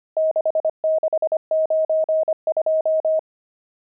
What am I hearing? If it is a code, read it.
Morse code, heard as a keyed tone, about 25 wpm, 630 Hz. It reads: 6692
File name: pangram-fs-25-cs-25-nums7